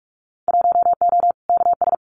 Morse code, heard as a keyed tone, about 46 wpm, 700 Hz. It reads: 1O KS
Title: Winter Field Day Exchanges 45wpm